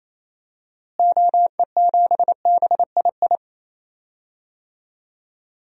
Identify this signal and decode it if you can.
Morse code, heard as a keyed tone, about 28 wpm, 700 Hz. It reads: OE7BII
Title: DXCC-100 Call Signs - 6 Characters - Repeated 3x - 28wpm